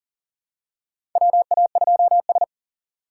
Morse code, heard as a keyed tone, about 40 wpm, 700 Hz. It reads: WA2S
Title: USA Call Signs - 4 Characters - 6x Character Spacing - 40wpm